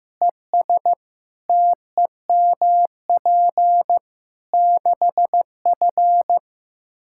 Morse code, heard as a keyed tone, about 15 wpm, 700 Hz. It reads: ES TEMP 6F
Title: Practice QSO - Exchange 2 - Info - 15wpm